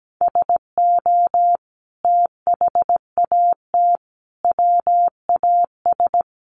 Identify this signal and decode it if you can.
Morse code, heard as a keyed tone, about 17 wpm, 700 Hz. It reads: SO THAT WAS
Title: Sentences from Top 100 Words 17wpm